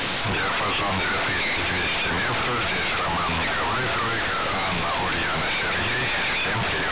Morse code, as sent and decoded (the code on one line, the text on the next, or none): none